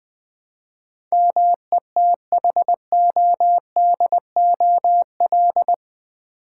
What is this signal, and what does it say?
Morse code, heard as a keyed tone, about 19 wpm, 700 Hz. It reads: METHODOL
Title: Build A Word - 4 Letters At A Time - 11-Letter Words - 20wpm